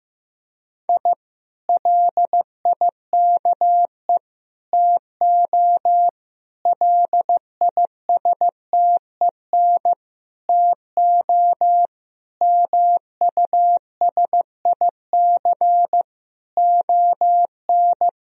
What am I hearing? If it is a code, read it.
Morse code, heard as a keyed tone, about 15 wpm, 700 Hz. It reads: I LIKE TO LISTEN TO MUSIC ON
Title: Sentences from Top 1000 Words - Review Part 3 - 15wpm